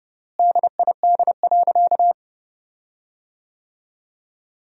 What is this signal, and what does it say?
Morse code, heard as a keyed tone, about 32 wpm, 700 Hz. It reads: DID.
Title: 4x Word-Spacing - Sentences from Top 100 Words - 30wpm